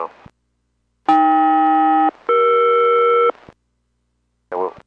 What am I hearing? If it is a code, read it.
Aircraft SELCAL, heard as two pairs of tones, with speech. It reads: AK-ER